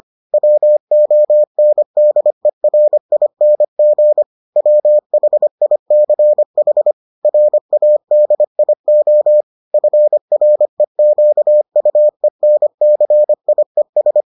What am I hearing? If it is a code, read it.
Morse code, heard as a keyed tone, about 25 wpm, 600 Hz. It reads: WONDERING WHICH RADIO FREQUENCIES